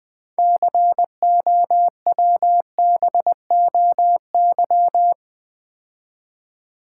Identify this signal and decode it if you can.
Morse code, heard as a keyed tone, about 20 wpm, 690 Hz. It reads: COWBOY
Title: Build A Word - 2 Letters At A Time - 6-Letter Words - 20wpm